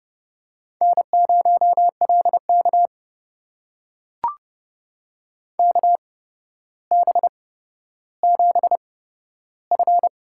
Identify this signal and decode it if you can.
Morse code, heard as a keyed tone, about 30 wpm, spaced as for 13 wpm, 700 Hz. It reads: N0LK KB7F
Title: USA Call Signs - 4 Characters - 8x Character Spacing - 30wpm